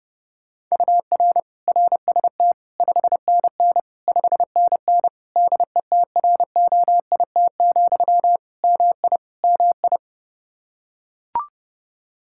Morse code, heard as a keyed tone, about 32 wpm, 700 Hz. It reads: UR RST 5NN 5NN DETROIT, MI MI
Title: Practice QSO - Exchange 1 - Beginning - 30wpm